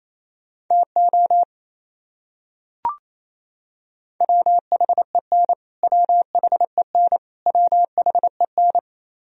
Morse code, heard as a keyed tone, about 28 wpm, 700 Hz. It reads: TO WHEN WHEN WHEN
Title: Top 100 Words Repeated 28wpm